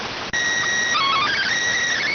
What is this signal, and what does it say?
SSTV (slow-scan television), PD 290.